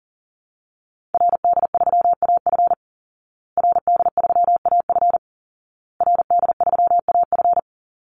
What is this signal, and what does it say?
Morse code, heard as a keyed tone, about 40 wpm, 700 Hz. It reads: RD3AF RD3AF RD3AF
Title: Call Signs Repeated 2x - 40wpm